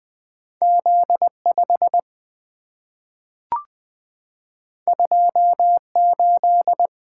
Morse code, heard as a keyed tone, about 20 wpm, 700 Hz. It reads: Z5 28
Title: Two Letter-Number Combinations - 20wpm